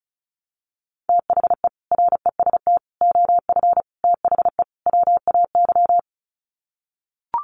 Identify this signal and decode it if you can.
Morse code, heard as a keyed tone, about 34 wpm, 700 Hz. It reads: THE REST OF THE WAY
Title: Sets of 5 Words Encore 35wpm